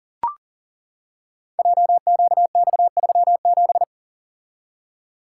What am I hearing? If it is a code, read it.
Morse code, heard as a keyed tone, about 41 wpm, 700 Hz. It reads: JQX37
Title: Code Groups - 5 Letter:Numbers - 40wpm